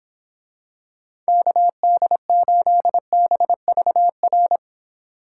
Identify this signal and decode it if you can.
Morse code, heard as a keyed tone, about 26 wpm, 700 Hz. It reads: KD8BVR